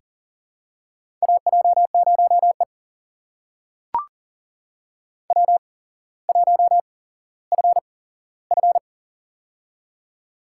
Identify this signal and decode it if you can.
Morse code, heard as a keyed tone, about 42 wpm, spaced as for 17 wpm, 700 Hz. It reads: AJ0E W1FF
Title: USA Call Signs - 4 Characters - 8x Character Spacing - 40wpm